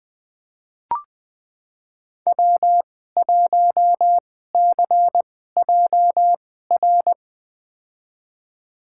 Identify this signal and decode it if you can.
Morse code, heard as a keyed tone, about 20 wpm, spaced as for 15 wpm, 700 Hz. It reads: W1CJR
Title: USA Call Signs - 5 Characters - 2x Character Spacing - 20wpm